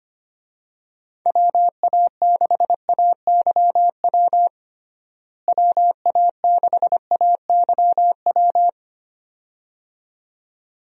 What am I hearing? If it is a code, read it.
Morse code, heard as a keyed tone, about 25 wpm, 700 Hz. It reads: WA6AYW WA6AYW
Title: USA Call Signs - 6 Characters - Repeated 2x - 25wpm